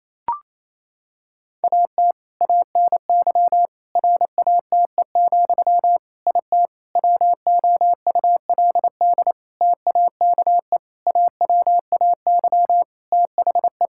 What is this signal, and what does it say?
Morse code, heard as a keyed tone, about 28 wpm, 700 Hz. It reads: AT ANY RATE, IT WOULD TAKE AWAY THE
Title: Sentences from Top 1600 Words 28wpm